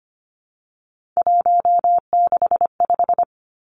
Morse code, heard as a keyed tone, about 25 wpm, 700 Hz. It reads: 165
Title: Three Challenging Letter-Number Combinations 25wpm